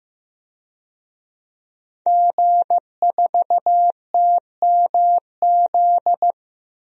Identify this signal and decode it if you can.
Morse code, heard as a keyed tone, about 15 wpm, 700 Hz. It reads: G4TMZ
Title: DXCC-100 Call Signs - 5 Characters - 15wpm